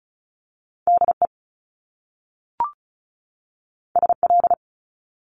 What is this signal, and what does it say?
Morse code, heard as a keyed tone, about 36 wpm, 700 Hz. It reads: DE SL
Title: Top Two Letter Combinations in Words 35wpm